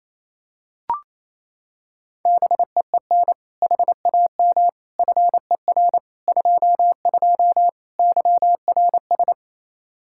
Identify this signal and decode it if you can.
Morse code, heard as a keyed tone, about 28 wpm, 700 Hz. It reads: BEEN HAM FER 22 YRS
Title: Practice QSO - Exchange 3 - More Info - 28wpm